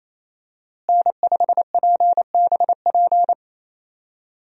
Morse code, heard as a keyed tone, about 28 wpm, 700 Hz. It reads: N5PBP
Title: USA Call Signs - 5 Characters - Repeated 3x - 28wpm